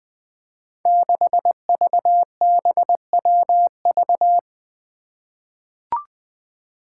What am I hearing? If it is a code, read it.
Morse code, heard as a keyed tone, about 20 wpm, 700 Hz. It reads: 6VBWV
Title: Five Challenging Letter-Number Combinations 20wpm